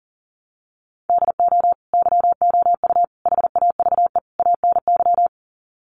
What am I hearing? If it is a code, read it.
Morse code, heard as a keyed tone, about 40 wpm, 700 Hz. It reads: DO YOU HAVE ANY
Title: Sets of 4 Words 40wpm